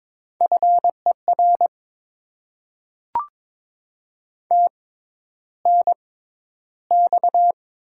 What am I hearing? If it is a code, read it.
Morse code, heard as a keyed tone, about 22 wpm, spaced as for 11 wpm, 700 Hz. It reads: FER TNX
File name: Top 20 QSO Elements - 6x Character Spacing - 22wpm